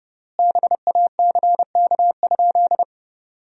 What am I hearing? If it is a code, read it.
Morse code, heard as a keyed tone, about 32 wpm, 700 Hz. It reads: BACK?
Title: 4x Word-Spacing - Sentences from Top 300 Words - 30wpm